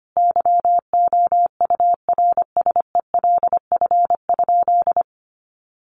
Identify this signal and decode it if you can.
Morse code, heard as a keyed tone, about 25 wpm, 700 Hz. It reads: YOURSELF?
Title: Sentences from Top 1200 Words 25wpm